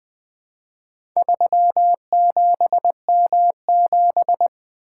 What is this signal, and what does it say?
Morse code, heard as a keyed tone, about 21 wpm, 700 Hz. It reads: 37M7
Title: Four Letter-Number Combinations - 20wpm